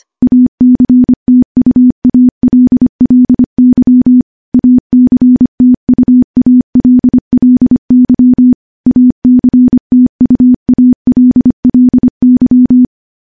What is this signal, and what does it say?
Morse code, heard as a keyed tone, about 25 wpm, 250 Hz. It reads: ACTUALLY ACTUALLY ACTUALLY
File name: morse / morse-training-25WPM